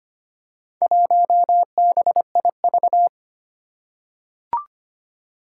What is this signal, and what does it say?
Morse code, heard as a keyed tone, about 25 wpm, 700 Hz. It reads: 1BIV
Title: Four Challenging Letter-Number Combinations 25wpm